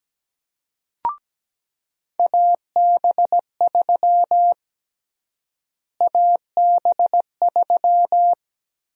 Morse code, heard as a keyed tone, about 17 wpm, 700 Hz. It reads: AB3 AB3
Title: USA Call Sign Prefixes - Repeated 2x - 17wpm